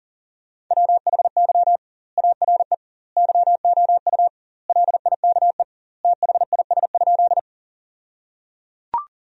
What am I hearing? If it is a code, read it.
Morse code, heard as a keyed tone, about 37 wpm, 700 Hz. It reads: WHY ARE YOU LIKE THIS?
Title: 2x Word-Spacing - Sentences from Top 200 Words - 40wpm